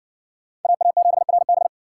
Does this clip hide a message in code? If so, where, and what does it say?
Yes: Morse code from 0.6 to 1.7 s, heard as a keyed tone, about 53 wpm, 700 Hz. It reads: AA7RB